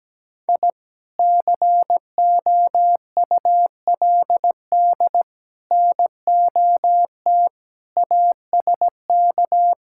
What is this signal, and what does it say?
Morse code, heard as a keyed tone, about 17 wpm, 700 Hz. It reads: I COULD NOT ASK
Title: Sentences from Top 1000 Words - Review Part 5 - 17wpm